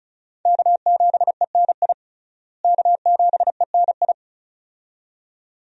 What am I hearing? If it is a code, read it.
Morse code, heard as a keyed tone, about 36 wpm, 700 Hz. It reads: K7ENI K7ENI
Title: USA Call Signs - 5 Characters - Repeated 2x - 35wpm